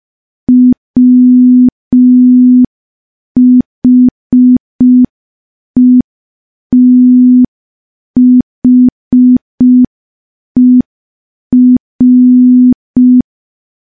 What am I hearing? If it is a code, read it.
Morse code, heard as a keyed tone, about 5 wpm, 250 Hz. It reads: WHETHER
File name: morse / morse-training-05WPM